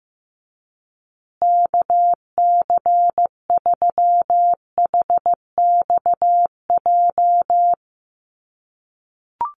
Morse code, heard as a keyed tone, about 15 wpm, 700 Hz. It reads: KC3HXJ
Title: USA Call Signs Repeated 3x 15wpm